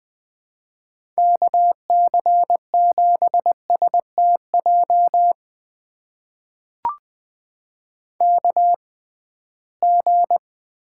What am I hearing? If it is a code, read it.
Morse code, heard as a keyed tone, about 20 wpm, spaced as for 15 wpm, 700 Hz. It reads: KC7STJ KG
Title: USA Call Signs - 6 Characters - 6x Character Spacing - 20wpm